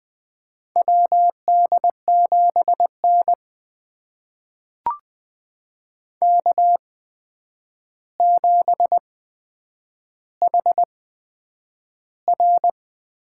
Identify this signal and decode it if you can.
Morse code, heard as a keyed tone, about 20 wpm, spaced as for 9 wpm, 700 Hz. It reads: WD7N K7HR
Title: USA Call Signs - 4 Characters - 8x Character Spacing - 20wpm